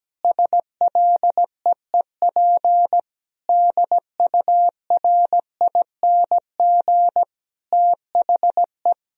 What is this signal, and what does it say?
Morse code, heard as a keyed tone, about 17 wpm, 700 Hz. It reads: SLEEP DURING THE
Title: Sentences from Top 1000 Words - Review Part 2 - 17wpm